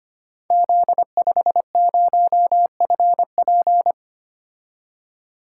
Morse code, heard as a keyed tone, about 25 wpm, 690 Hz. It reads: Z50FP
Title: Code Groups - 5 Letter:Numbers - 25wpm